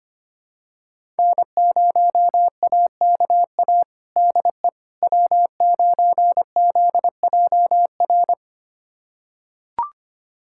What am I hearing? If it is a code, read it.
Morse code, heard as a keyed tone, about 25 wpm, 700 Hz. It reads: N0AKA DE W9ZJR
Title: USA Callsign DE Callsign - 5 Characters - 25wpm